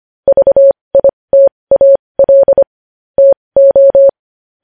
Morse code, heard as a keyed tone, about 25 wpm, spaced as for 20 wpm, 550 Hz. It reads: VITAL TO
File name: podcast / News headlines-1x-25x20